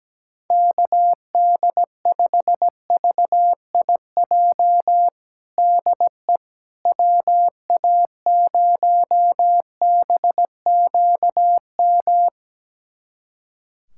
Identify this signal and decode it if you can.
Morse code, heard as a keyed tone, about 17 wpm, 700 Hz. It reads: KD5VIJ DE WA0BQM
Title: USA Callsign DE Callsign - 6 Characters - 17wpm